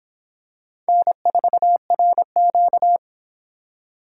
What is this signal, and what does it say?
Morse code, heard as a keyed tone, about 26 wpm, 700 Hz. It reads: N4RQ